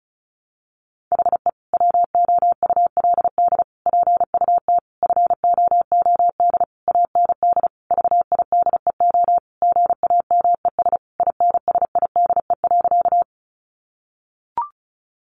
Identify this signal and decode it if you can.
Morse code, heard as a keyed tone, about 34 wpm, 700 Hz. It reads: HE WOULD PUT FOOD AND VIDEO GAMES INSIDE.
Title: Easy Phrases 35wpm